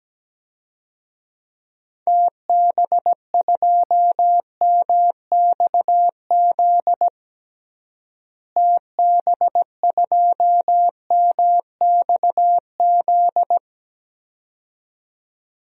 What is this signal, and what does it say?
Morse code, heard as a keyed tone, about 17 wpm, 700 Hz. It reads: TB2MXZ TB2MXZ
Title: DXCC-100 Call Signs - 6 Characters - Repeated 2x - 17wpm